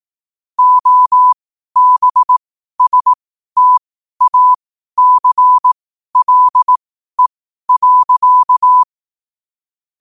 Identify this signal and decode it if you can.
Morse code, heard as a keyed tone, about 18 wpm, spaced as for 12 wpm, 1000 Hz. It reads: OBSTACLE.
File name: qotdfiles / 12wpm